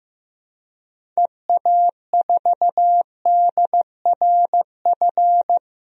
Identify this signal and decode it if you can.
Morse code, heard as a keyed tone, about 15 wpm, 700 Hz. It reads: EA4DRF
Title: DXCC-100 Call Signs - 6 Characters - 15wpm